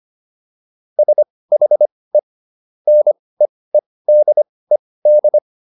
Morse code, heard as a keyed tone, about 25 wpm, spaced as for 18 wpm, 600 Hz. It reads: SHE NEEDED